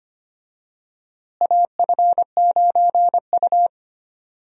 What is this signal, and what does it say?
Morse code, heard as a keyed tone, about 25 wpm, 700 Hz. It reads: AF9U
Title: USA Call Signs - 4 Characters - Repeated 2x - 25wpm